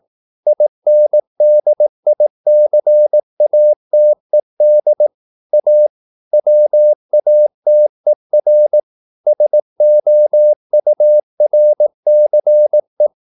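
Morse code, heard as a keyed tone, about 18 wpm, 600 Hz. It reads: INDICATED A WATER SOURCE
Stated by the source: Morse practice at 18 wpm, English text, from CW med Gnister.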